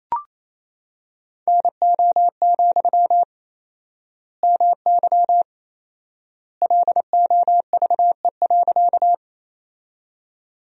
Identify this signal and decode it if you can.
Morse code, heard as a keyed tone, about 28 wpm, 700 Hz. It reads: NO, MY LOVE.
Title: 4x Word-Spacing - Sentences from Top 400 Words - 28wpm